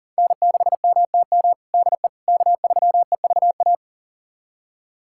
Morse code, heard as a keyed tone, about 40 wpm, 700 Hz. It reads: N6MTM DE K3EVA
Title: USA Callsign DE Callsign - 5 Characters - 40wpm